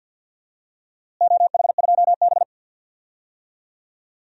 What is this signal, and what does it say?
Morse code, heard as a keyed tone, about 51 wpm, 700 Hz. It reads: OH2B
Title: DXCC-100 Call Signs - 4 Characters - Repeated 2x - 50wpm